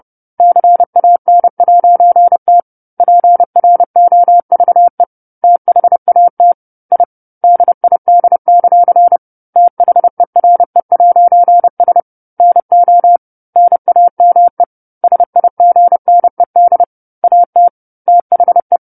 Morse code, heard as a keyed tone, about 30 wpm, 700 Hz. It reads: CAN'T PROVE THAT I DID; THERE'S NO NAME SIGNED AT THE